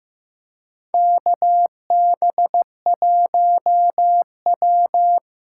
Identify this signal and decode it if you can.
Morse code, heard as a keyed tone, about 15 wpm, 700 Hz. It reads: KB1W
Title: USA Call Signs - 6 Characters - 8x Character Spacing - 15wpm